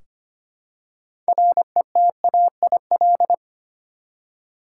Morse code, heard as a keyed tone, about 25 wpm, 700 Hz. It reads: RETAIL
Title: Practice QSO - Exchange 3 - More Info - 25wpm